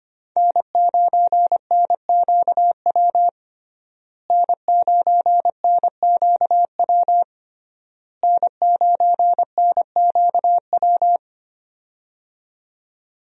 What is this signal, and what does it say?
Morse code, heard as a keyed tone, about 25 wpm, 700 Hz. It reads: N9NQW N9NQW N9NQW
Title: USA Call Signs - 5 Characters - Repeated 3x - 25wpm